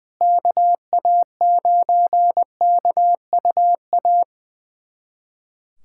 Morse code, heard as a keyed tone, about 20 wpm, 700 Hz. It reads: KA9KUA
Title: USA Callsign DE Callsign 20wpm